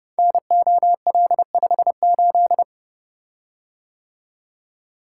Morse code, heard as a keyed tone, about 30 wpm, 700 Hz. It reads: NOL58
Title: Code Groups - 5 Letter:Numbers - 30wpm